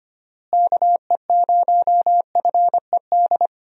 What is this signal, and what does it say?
Morse code, heard as a keyed tone, about 25 wpm, 700 Hz. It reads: KE0FED
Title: USA Call Signs - 6 Characters - Repeated 3x - 25wpm